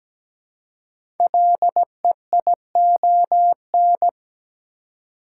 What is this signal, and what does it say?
Morse code, heard as a keyed tone, about 17 wpm, 700 Hz. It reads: LEION
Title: Code Groups - 5 Letter:Numbers - 17wpm